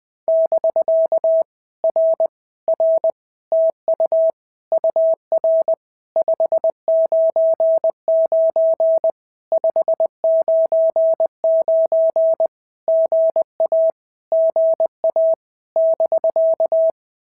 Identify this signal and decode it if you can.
Morse code, heard as a keyed tone, about 19 wpm, 650 Hz. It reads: <BK> R R TU UR 599 599 GA GA <BK>
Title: POTA QSO Practice - 20wpm